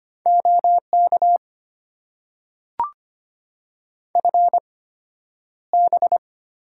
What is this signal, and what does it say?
Morse code, heard as a keyed tone, about 25 wpm, spaced as for 11 wpm, 700 Hz. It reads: OK FB
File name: Top 90 QSO Elements - 8x Character Spacing - 25wpm